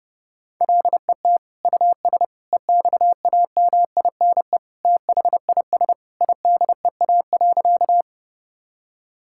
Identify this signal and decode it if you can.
Morse code, heard as a keyed tone, about 30 wpm, 700 Hz. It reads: LET US EXAMINE THIS IDEA.
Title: Sentences from Top 1600 Words 30wpm